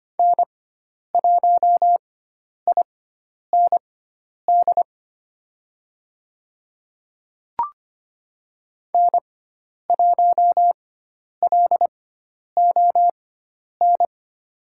Morse code, heard as a keyed tone, about 25 wpm, spaced as for 10 wpm, 700 Hz. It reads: N1IND N1LON
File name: Lesson 16g - Letter c - Call Signs 25wpm